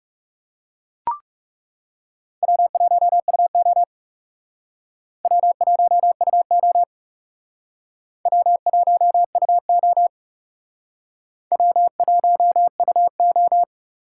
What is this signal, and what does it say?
Morse code, heard as a keyed tone, about 38 wpm, 700 Hz. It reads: W1UO W1UO W1UO W1UO